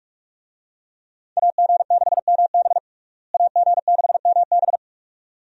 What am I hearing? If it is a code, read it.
Morse code, heard as a keyed tone, about 45 wpm, 700 Hz. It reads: AG6MB AG6MB
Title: USA Call Signs - 5 Characters - Repeated 2x - 45wpm